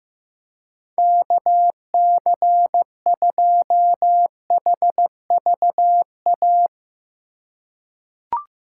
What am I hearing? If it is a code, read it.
Morse code, heard as a keyed tone, about 15 wpm, 700 Hz. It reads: KC2HVA
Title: USA Call Signs - 6 Characters - 4x Character Spacing - 15wpm